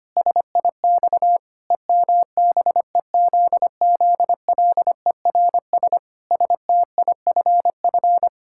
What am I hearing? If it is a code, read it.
Morse code, heard as a keyed tone, about 25 wpm, 700 Hz. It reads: SIX EMBEZZLERS STIFF
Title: Pangram Challenge v2 - 25wpm